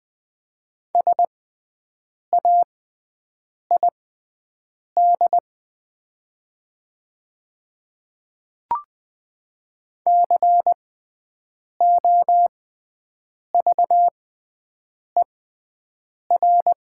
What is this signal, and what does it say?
Morse code, heard as a keyed tone, about 20 wpm, spaced as for 7 wpm, 700 Hz. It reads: SAID COVER
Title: 6x Character Spacing - Top 300 Words - 20wpm